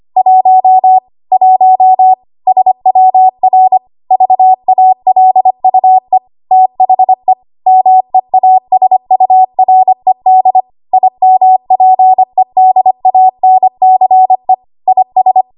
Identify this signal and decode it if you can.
Morse code, heard as a keyed tone, about 25 wpm, 750 Hz. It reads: 1 1 SWR VALUE THE MEASURED IMPEDANCE IS
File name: Archive / 25 WPM / 25WPM